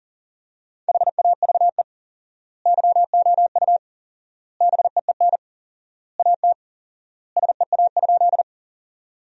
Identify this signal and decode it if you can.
Morse code, heard as a keyed tone, about 40 wpm, 700 Hz. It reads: HAVE YOU BEEN AT SEA?
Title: 4x Word-Spacing - Sentences from Top 300 Words - 40wpm